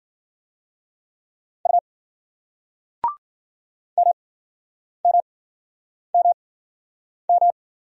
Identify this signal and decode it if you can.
Morse code, heard as a keyed tone, about 55 wpm, spaced as for 9 wpm, 700 Hz. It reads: U MMMM